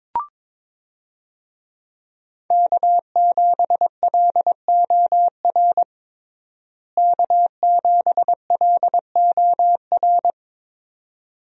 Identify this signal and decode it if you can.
Morse code, heard as a keyed tone, about 22 wpm, 700 Hz. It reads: K7LOR K7LOR
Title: USA Call Signs - 5 Characters - Repeated 2x - 22wpm